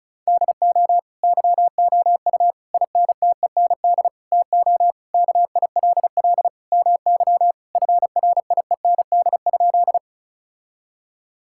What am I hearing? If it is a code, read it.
Morse code, heard as a keyed tone, about 36 wpm, 700 Hz. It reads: DO YOU INTEND TO KILL MY FRIEND?
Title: Sentences from Top 1600 Words 35wpm